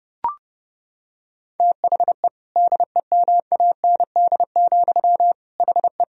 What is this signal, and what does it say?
Morse code, heard as a keyed tone, about 32 wpm, 700 Hz. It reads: THE DEMAND, HE
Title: Sentences from Top 1000 Words - Review Part 4 - 30wpm